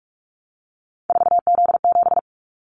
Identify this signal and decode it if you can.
Morse code, heard as a keyed tone, about 44 wpm, 700 Hz. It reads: 4Z7
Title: Callsign Prefixes for Top 50 DXCC - 45wpm